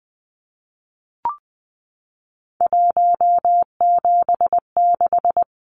Morse code, heard as a keyed tone, about 20 wpm, 700 Hz. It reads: 176
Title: Numbers - 3-Digit 20wpm